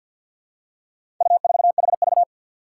Morse code, heard as a keyed tone, about 47 wpm, 700 Hz. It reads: U4HV
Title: Four Challenging Letter-Number Combinations 50wpm